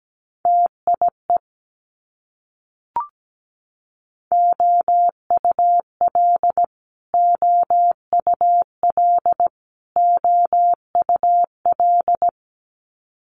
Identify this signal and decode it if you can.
Morse code, heard as a keyed tone, about 17 wpm, 700 Hz. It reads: TIE OUL OUL OUL
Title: Multi-letter Phonemes Repeated 3x 17wpm